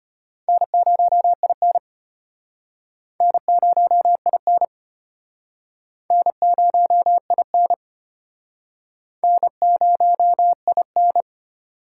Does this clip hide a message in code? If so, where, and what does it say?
Yes: Morse code from 0.5 to 11.2 s, heard as a keyed tone, about 33 wpm, 700 Hz. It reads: N0IN N0IN N0IN N0IN